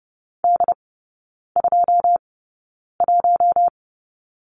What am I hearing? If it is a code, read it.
Morse code, heard as a keyed tone, about 30 wpm, spaced as for 9 wpm, 700 Hz. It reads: D21
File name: Lesson 19e - Letter w - Three Characters 30wpm